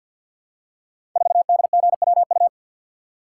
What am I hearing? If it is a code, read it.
Morse code, heard as a keyed tone, about 51 wpm, 700 Hz. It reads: 4DGWU
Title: Code Groups - 5 Letter:Numbers - 50wpm